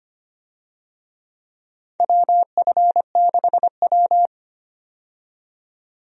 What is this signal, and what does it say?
Morse code, heard as a keyed tone, about 25 wpm, 700 Hz. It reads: WF6W
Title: USA Call Signs - 4 Characters - 25wpm